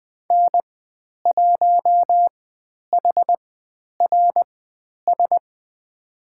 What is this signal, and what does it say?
Morse code, heard as a keyed tone, about 20 wpm, spaced as for 10 wpm, 700 Hz. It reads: N1HRS
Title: 109 Lesson 18g - Letter m - Call Signs 20wpm